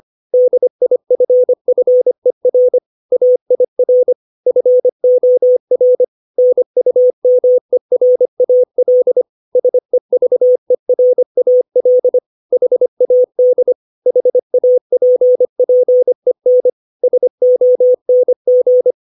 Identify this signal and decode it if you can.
Morse code, heard as a keyed tone, about 25 wpm, 500 Hz. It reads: DIFFER AIR FOR NUMERAL SEVERAL HAD HAPPEN SONG
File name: 632 words / 632 Words 25WPM